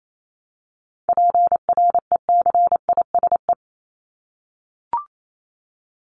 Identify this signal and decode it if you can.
Morse code, heard as a keyed tone, about 28 wpm, 700 Hz. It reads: PRECISE
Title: LY Suffix - Top 100 Words with Base Words 28wpm